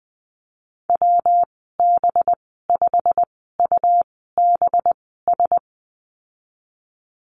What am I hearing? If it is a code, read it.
Morse code, heard as a keyed tone, about 20 wpm, spaced as for 15 wpm, 700 Hz. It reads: WB5UBS
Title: 207 Lesson 33g - Letter j - Call Signs 20wpm